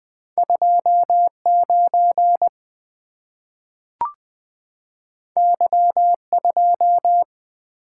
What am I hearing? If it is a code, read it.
Morse code, heard as a keyed tone, about 20 wpm, 700 Hz. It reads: 2 9 Y 2
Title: Two Letter-Number Combinations - 20wpm